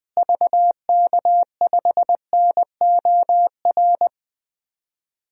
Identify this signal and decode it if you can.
Morse code, heard as a keyed tone, about 20 wpm, 700 Hz. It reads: VK5NOR
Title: DXCC-100 Call Signs - 6 Characters - Repeated 3x - 20wpm